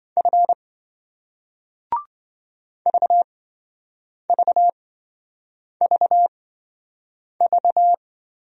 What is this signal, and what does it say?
Morse code, heard as a keyed tone, about 27 wpm, spaced as for 7 wpm, 700 Hz. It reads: F VVVV